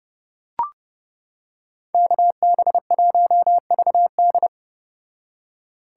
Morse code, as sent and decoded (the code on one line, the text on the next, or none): -.- -... .---- ...- -..
KB1VD